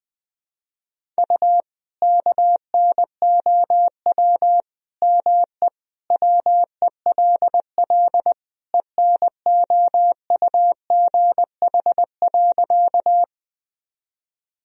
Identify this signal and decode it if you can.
Morse code, heard as a keyed tone, about 20 wpm, 700 Hz. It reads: U KNOW ME WELL ENOUGH.
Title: Sentences from Top 400 Words and Abbreviations 20wpm